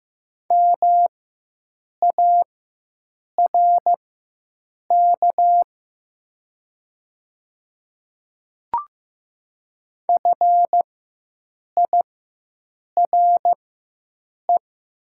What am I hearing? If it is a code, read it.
Morse code, heard as a keyed tone, about 15 wpm, spaced as for 7 wpm, 700 Hz. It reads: MARK FIRE
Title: 4x Character Spacing - Top 500 Words - 15wpm